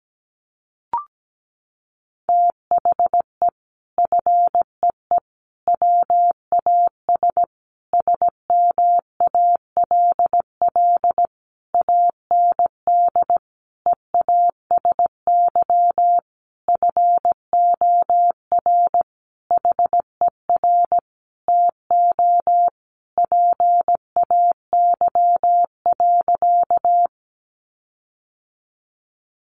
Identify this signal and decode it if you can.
Morse code, heard as a keyed tone, about 17 wpm, 700 Hz. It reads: THE FEE WAS SMALL AND EASY FOR HER TO PAY.
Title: Sentences from Top 1500 Words 17wpm